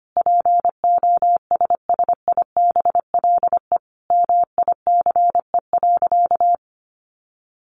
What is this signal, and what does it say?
Morse code, heard as a keyed tone, about 25 wpm, 700 Hz. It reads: POSSIBLE MICE.
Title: A Sound Of Thunder 25wpm